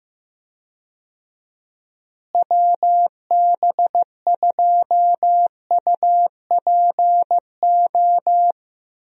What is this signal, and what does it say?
Morse code, heard as a keyed tone, about 15 wpm, 700 Hz. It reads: WB2UPO
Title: USA Call Signs - 6 Characters - Repeated 3x - 15wpm